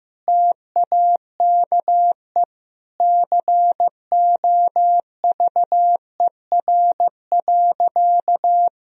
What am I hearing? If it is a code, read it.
Morse code, heard as a keyed tone, about 15 wpm, 700 Hz. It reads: TAKE COVER.
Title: Sentences from Top 1000 Words - Review Part 1 - 15wpm